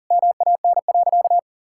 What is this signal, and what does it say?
Morse code, heard as a keyed tone, about 40 wpm, 700 Hz. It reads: MAN.
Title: Sentences from Top 1000 Words - Review Part 6 - 40wpm